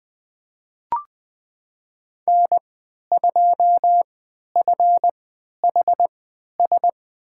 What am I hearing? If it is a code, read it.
Morse code, heard as a keyed tone, about 20 wpm, spaced as for 11 wpm, 700 Hz. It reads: N2FHS
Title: USA Call Signs - 5 Characters - 3x Character Spacing - 20wpm